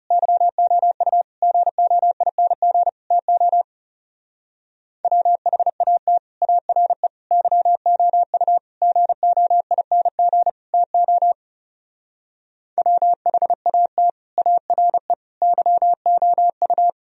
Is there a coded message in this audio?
Yes: Morse code from 0.1 to 16.9 s, heard as a keyed tone, about 38 wpm, 700 Hz. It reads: YOU GOING TO WHAT ARE YOU GOING TO WHAT ARE YOU